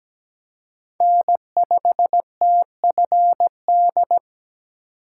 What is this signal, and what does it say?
Morse code, heard as a keyed tone, about 20 wpm, 700 Hz. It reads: N5TFD
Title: Speed Racing - USA 1x3 Call Signs 17wpm